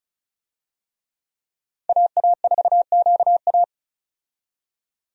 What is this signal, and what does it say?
Morse code, heard as a keyed tone, about 34 wpm, 700 Hz. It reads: AA4QA
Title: USA Call Signs - 5 Characters - 35wpm